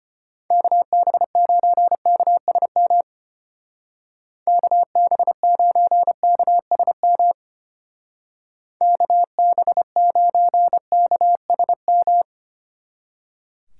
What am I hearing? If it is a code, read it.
Morse code, heard as a keyed tone, about 30 wpm, 700 Hz. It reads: KB9KSM KB9KSM KB9KSM
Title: Speed Racing - USA 2x3 Call Signs 25wpm